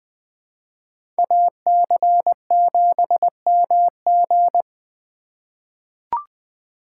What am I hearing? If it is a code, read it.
Morse code, heard as a keyed tone, about 21 wpm, 700 Hz. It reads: AC7MG
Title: USA Call Signs - 5 Characters - 4x Character Spacing - 20wpm